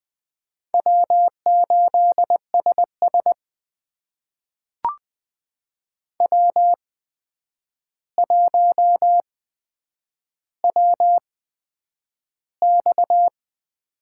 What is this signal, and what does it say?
Morse code, heard as a keyed tone, about 20 wpm, spaced as for 9 wpm, 700 Hz. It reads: W8SS W1WX
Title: USA Call Signs - 4 Characters - 8x Character Spacing - 20wpm